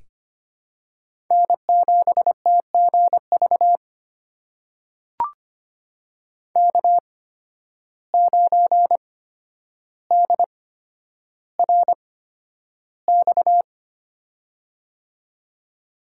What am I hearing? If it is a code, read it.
Morse code, heard as a keyed tone, about 25 wpm, spaced as for 11 wpm, 700 Hz. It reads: N7TGV K9DRX
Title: USA Call Signs - 5 Characters - 8x Character Spacing - 25wpm